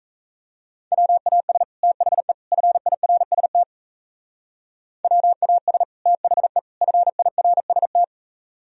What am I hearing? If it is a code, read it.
Morse code, heard as a keyed tone, about 32 wpm, 700 Hz. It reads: WAS THE FIRST WAS THE FIRST
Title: Speed-Racing - Sets of 3 Words 28wpm